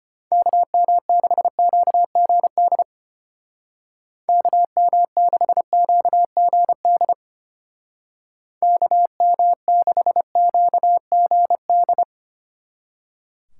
Morse code, heard as a keyed tone, about 30 wpm, 700 Hz. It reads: KM6QGD KM6QGD KM6QGD
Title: Speed Racing - USA 2x3 Call Signs 25wpm